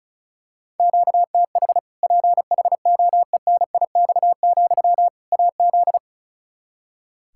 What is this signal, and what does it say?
Morse code, heard as a keyed tone, about 36 wpm, 700 Hz. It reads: QTH PHOENIX, AZ
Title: Practice QSO - Exchange 1 - Beginning - 35wpm